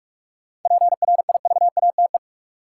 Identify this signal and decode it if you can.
Morse code, heard as a keyed tone, about 46 wpm, 700 Hz. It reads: PRIVATE
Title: Sentences from Top 1000 Words - New Word Drill - 45wpm